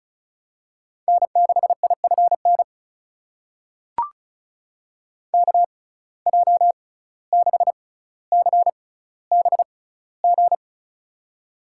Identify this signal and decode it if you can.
Morse code, heard as a keyed tone, about 34 wpm, spaced as for 17 wpm, 700 Hz. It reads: N6IFN KJ6CBG
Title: USA Call Signs - 6x Character Spacing - 35wpm